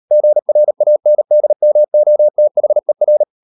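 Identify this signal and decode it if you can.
Morse code, heard as a keyed tone, about 38 wpm, 600 Hz. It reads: GRANDMOTHER
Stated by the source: Morse code practice at 38 WPM, sending Ken Follett's novel Never in English, from CW med Gnister.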